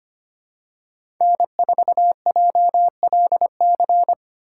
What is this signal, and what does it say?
Morse code, heard as a keyed tone, about 25 wpm, 700 Hz. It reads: N4JLC
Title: USA Call Signs - 5 Characters - 4x Character Spacing - 25wpm